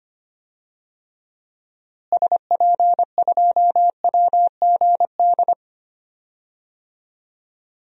Morse code, heard as a keyed tone, about 25 wpm, 700 Hz. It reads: SP2WGD
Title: DXCC-100 Call Signs - 6 Characters - Repeated 2x - 25wpm